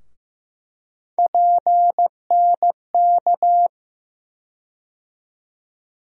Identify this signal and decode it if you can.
Morse code, heard as a keyed tone, about 16 wpm, 700 Hz. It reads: PNK
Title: USA Call Sign Postfixes - 3 Letters 15wpm